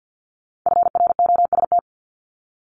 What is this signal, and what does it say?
Morse code, heard as a keyed tone, about 50 wpm, 690 Hz. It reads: FROST
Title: Multi-letter Phonemes 50wpm